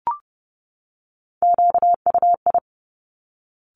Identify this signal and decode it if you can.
Morse code, heard as a keyed tone, about 30 wpm, 700 Hz. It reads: QUI
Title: Top Three Letter Combinations in Words 30wpm